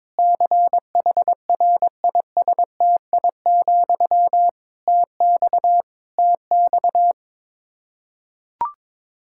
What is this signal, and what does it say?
Morse code, heard as a keyed tone, about 22 wpm, 700 Hz. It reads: CHRISTI, TX TX
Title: K1USN SST QSO Practice 22wpm